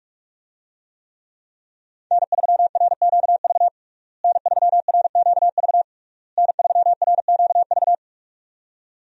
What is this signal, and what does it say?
Morse code, heard as a keyed tone, about 46 wpm, 700 Hz. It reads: N3RQV N3RQV N3RQV
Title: USA Call Signs - 5 Characters - Repeated 3x - 45wpm